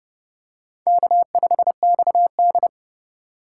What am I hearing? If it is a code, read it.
Morse code, heard as a keyed tone, about 27 wpm, 700 Hz. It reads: K5XD